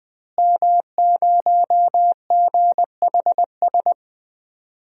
Morse code, heard as a keyed tone, about 20 wpm, 700 Hz. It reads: M0GHS
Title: DXCC-100 Call Signs - 5 Characters - Repeated 2x - 20wpm